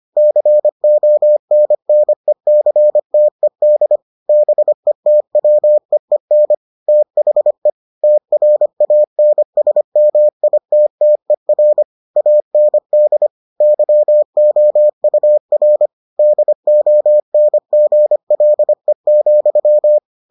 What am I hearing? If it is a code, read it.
Morse code, heard as a keyed tone, about 25 wpm, 600 Hz. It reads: CONNECTED BETWEEN THE TRANSMITTER AND YOUR DONGLE,